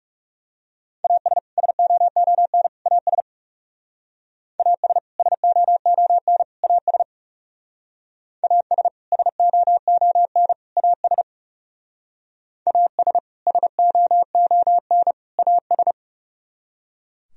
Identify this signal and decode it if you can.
Morse code, heard as a keyed tone, about 39 wpm, 700 Hz. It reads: AS SOON AS AS SOON AS AS SOON AS AS SOON AS